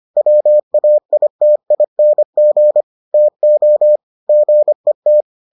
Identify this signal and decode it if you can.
Morse code, heard as a keyed tone, about 25 wpm, 600 Hz. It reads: WAITING TO GET